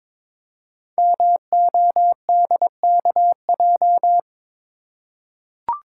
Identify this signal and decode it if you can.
Morse code, heard as a keyed tone, about 22 wpm, 700 Hz. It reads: MODKJ
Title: Code Groups - 5 Letter:Numbers - 22wpm